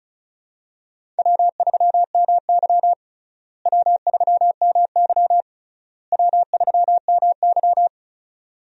Morse code, heard as a keyed tone, about 36 wpm, 700 Hz. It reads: W3MY W3MY W3MY
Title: USA Call Signs - 4 Characters - Repeated 3x - 35wpm